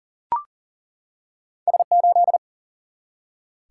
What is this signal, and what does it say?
Morse code, heard as a keyed tone, about 38 wpm, 700 Hz. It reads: S8
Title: Single-Letter Single-Number 40wpm